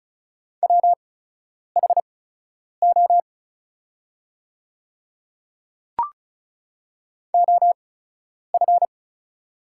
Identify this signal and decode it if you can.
Morse code, heard as a keyed tone, about 36 wpm, spaced as for 10 wpm, 700 Hz. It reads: WHO OF
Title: 8x Character Spacing - Top 200 Words - 35wpm